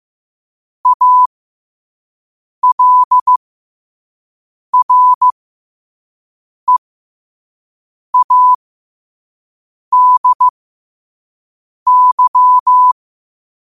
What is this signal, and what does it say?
Morse code, heard as a keyed tone, about 15 wpm, spaced as for 5 wpm, 1000 Hz. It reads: ALREADY
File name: qotdfiles / 5wpm